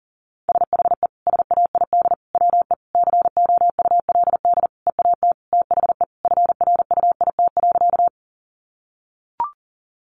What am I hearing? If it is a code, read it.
Morse code, heard as a keyed tone, about 40 wpm, 700 Hz. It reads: SHE SAID WE COULD EAT THE FRUIT.
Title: Sentences from Top 1300 Words 40wpm